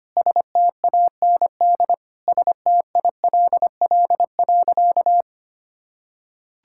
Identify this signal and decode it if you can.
Morse code, heard as a keyed tone, about 25 wpm, 700 Hz. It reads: STAND STILL.
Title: Sentences from Top 1000 Words - Review Part 6 - 25wpm